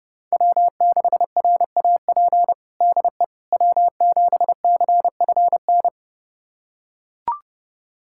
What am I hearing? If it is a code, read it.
Morse code, heard as a keyed tone, about 32 wpm, 700 Hz. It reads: W6RAP DE W7CFN
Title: USA Callsign DE Callsign - 5 Characters - 30wpm